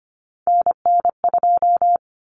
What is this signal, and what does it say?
Morse code, heard as a keyed tone, about 25 wpm, 700 Hz. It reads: NN2
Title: Common Cut Numbers - 3-Digit 25wpm